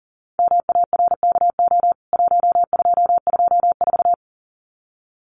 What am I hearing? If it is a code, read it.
Morse code, heard as a keyed tone, about 40 wpm, 700 Hz. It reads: MARKO 1224
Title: CWT Exchange 40wpm